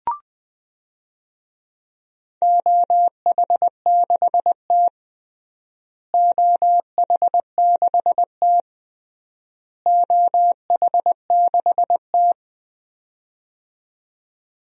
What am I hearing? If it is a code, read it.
Morse code, heard as a keyed tone, about 21 wpm, 700 Hz. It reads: OH6T OH6T OH6T
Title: DXCC-100 Call Signs - 4 Characters - Repeated 3x - 20wpm